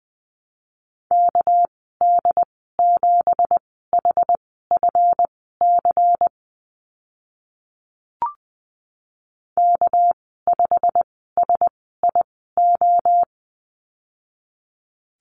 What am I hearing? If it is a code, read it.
Morse code, heard as a keyed tone, about 20 wpm, spaced as for 14 wpm, 700 Hz. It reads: KD7HFC K5SIO
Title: 213 Lesson 34g - Number 8 - Call Signs 20wpm